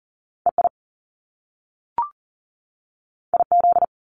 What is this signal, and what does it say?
Morse code, heard as a keyed tone, about 40 wpm, 700 Hz. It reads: EI IZ
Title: Top Two Letter Combinations in Words 40wpm